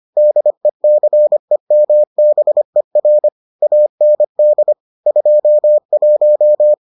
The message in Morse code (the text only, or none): DECEMBER AND 21